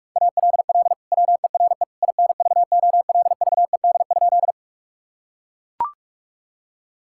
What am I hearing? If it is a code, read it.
Morse code, heard as a keyed tone, about 42 wpm, 700 Hz. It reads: ALL WERE INVOLVED?
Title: Sentences from Top 1000 Words - Review Part 5 - 45wpm